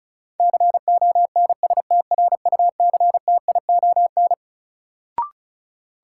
Morse code, heard as a keyed tone, about 36 wpm, 700 Hz. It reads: CONSTRUCTION
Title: Sentences from Top 1600 Words - New Word Drill - 35wpm